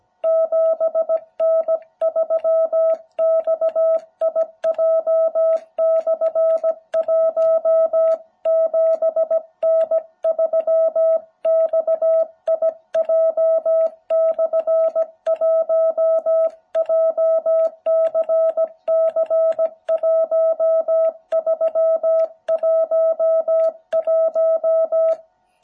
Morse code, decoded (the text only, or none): 7N3XIJ/1 7N3XIJ/1 JCC1311